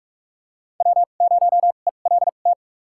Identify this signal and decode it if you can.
Morse code, heard as a keyed tone, about 45 wpm, spaced as for 32 wpm, 700 Hz. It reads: W0ELT
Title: USA Call Signs - 5 Characters - 2x Character Spacing - 45wpm